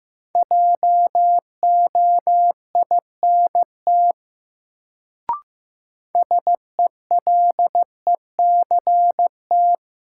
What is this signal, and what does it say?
Morse code, heard as a keyed tone, about 15 wpm, 700 Hz. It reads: JOINT SELECT
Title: Sentences from Top 1300 Words - New Word Drill - 15wpm